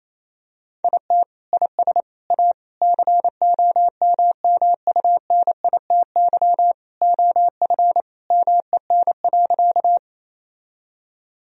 Morse code, heard as a keyed tone, about 28 wpm, 700 Hz. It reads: IT IS A COMMUNITY OF MEN.
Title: Sentences from Top 1000 Words - Review Part 4 - 28wpm